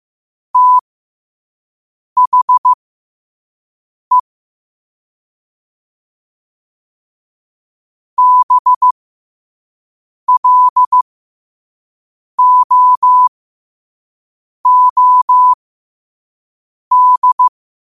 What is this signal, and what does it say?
Morse code, heard as a keyed tone, about 15 wpm, spaced as for 5 wpm, 1000 Hz. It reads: THE BLOOD
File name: qotdfiles / 5wpm